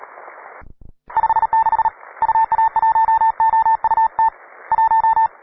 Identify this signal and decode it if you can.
Morse code, heard as a keyed tone, about 39 wpm, 890 Hz, with radio noise. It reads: 56 UA1OUT 1
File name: wae / cw pro / rt9w cw